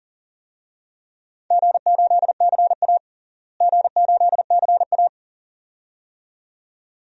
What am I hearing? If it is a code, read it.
Morse code, heard as a keyed tone, about 40 wpm, 700 Hz. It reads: G8CA G8CA
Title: DXCC-100 Call Signs - 4 Characters - Repeated 2x - 40wpm